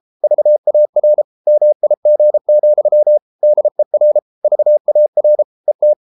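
Morse code, heard as a keyed tone, about 33 wpm, 600 Hz. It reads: VAR MIG, DER VAR ET
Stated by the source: Morse code at 33 wpm, QRQ.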